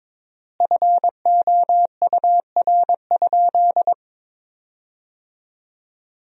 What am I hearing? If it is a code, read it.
Morse code, heard as a keyed tone, about 23 wpm, 700 Hz. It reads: FOUR?
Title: 4x Word-Spacing - Sentences from Top 300 Words - 22wpm